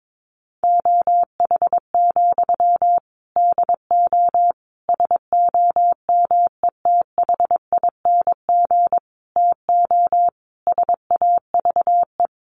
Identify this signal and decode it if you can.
Morse code, heard as a keyed tone, about 22 wpm, 700 Hz. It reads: OH, DO SOMETHING TO SAVE
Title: Sentences from Top 1100 Words 22wpm